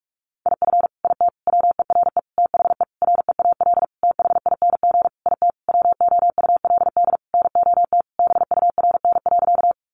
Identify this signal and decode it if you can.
Morse code, heard as a keyed tone, about 44 wpm, 700 Hz. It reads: IF IT WERE THE REAL THING IT WOULD NOT BURN.
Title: Sentences from Top 1300 Words 45wpm